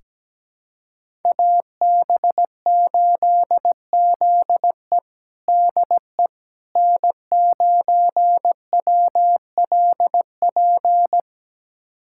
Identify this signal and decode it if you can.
Morse code, heard as a keyed tone, about 17 wpm, 700 Hz. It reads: AB8ZE DE N9WLP
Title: USA Callsign DE Callsign - 5 Characters - 17wpm